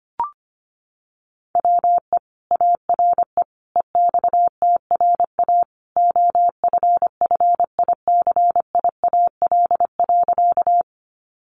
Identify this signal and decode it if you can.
Morse code, heard as a keyed tone, about 25 wpm, 700 Hz. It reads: WE ARE EXTRA OFFICIAL.
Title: Sentences from Top 1000 Words 25wpm